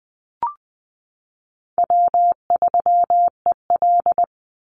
Morse code, heard as a keyed tone, about 20 wpm, 700 Hz. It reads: W3EL
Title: Call Signs - 20wpm